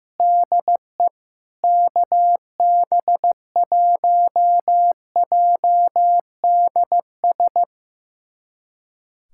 Morse code, heard as a keyed tone, about 15 wpm, 700 Hz. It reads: DE KB1JDS
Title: USA Callsign DE Callsign 15wpm